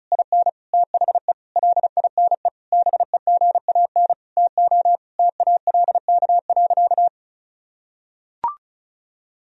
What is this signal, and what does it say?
Morse code, heard as a keyed tone, about 34 wpm, 700 Hz. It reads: IN THE LINE BEGAN TO TALK.
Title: Sentences from Top 1000 Words - Review Part 1 - 35wpm